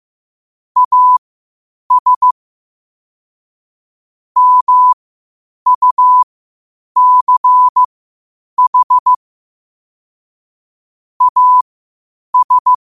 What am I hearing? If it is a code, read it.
Morse code, heard as a keyed tone, about 15 wpm, spaced as for 8 wpm, 1000 Hz. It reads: AS MUCH AS
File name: qotdfiles / 8wpm